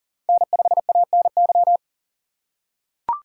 Morse code, heard as a keyed tone, about 37 wpm, 700 Hz. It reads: N5ANY
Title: USA Call Signs - 3x Character Spacing - 40wpm